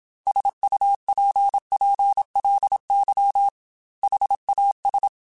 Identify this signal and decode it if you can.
Morse code, heard as a keyed tone, about 27 wpm, 800 Hz. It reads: SUPPLY HAS
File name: Technical Subjects / Technical Subjects-mp3s / Sw Pwr Sup @25